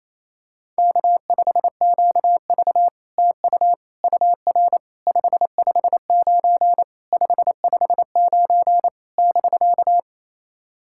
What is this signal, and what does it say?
Morse code, heard as a keyed tone, about 28 wpm, 660 Hz. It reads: K5QV TU UR 559 559 <BK>
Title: SOTA QSO Practice 28wpm